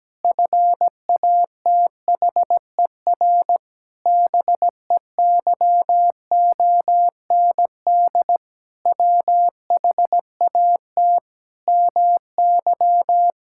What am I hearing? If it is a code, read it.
Morse code, heard as a keyed tone, about 17 wpm, 700 Hz. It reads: FATHER BEYOND WHAT MY
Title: Sentences from Top 1000 Words - Review Part 6 - 17wpm